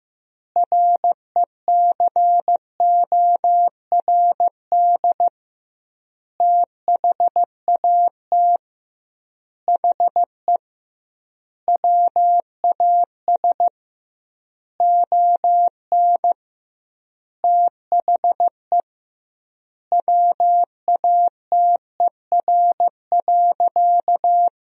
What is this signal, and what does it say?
Morse code, heard as a keyed tone, about 15 wpm, 700 Hz. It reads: RECORD THAT HE WAS ON THE WATER.
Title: 2x Word-Spacing - Sentences from Top 500 Words - 15wpm